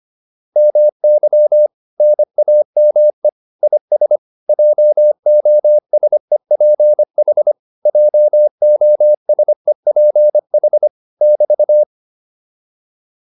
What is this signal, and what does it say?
Morse code, heard as a keyed tone, about 25 wpm, 600 Hz. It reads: MY NAME IS JOSEPH JOSEPH <BT>